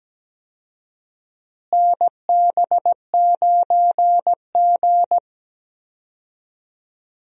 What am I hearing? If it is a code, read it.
Morse code, heard as a keyed tone, about 17 wpm, 700 Hz. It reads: NB9G
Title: USA Call Signs - 4 Characters - Repeated 3x - 17wpm